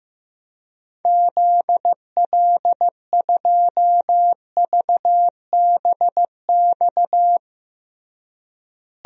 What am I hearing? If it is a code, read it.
Morse code, heard as a keyed tone, about 15 wpm, 700 Hz. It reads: ZL2VBX
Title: DXCC-100 Call Signs - 6 Characters - Repeated 2x - 15wpm